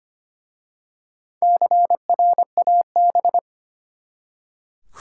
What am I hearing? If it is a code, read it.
Morse code, heard as a keyed tone, about 25 wpm, 700 Hz. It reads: CRAB
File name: practice / 25wpm / Easy Words with Hints - 25wpm